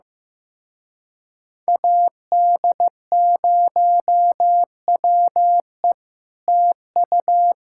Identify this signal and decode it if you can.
Morse code, heard as a keyed tone, about 15 wpm, 700 Hz. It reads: AD0WE TU
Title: Speed-Racing Common QSO Phrases Encore - 15wpm